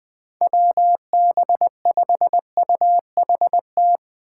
Morse code, heard as a keyed tone, about 20 wpm, 700 Hz. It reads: WB5UHT
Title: USA Call Signs - 6 Characters - 2x Character Spacing - 20wpm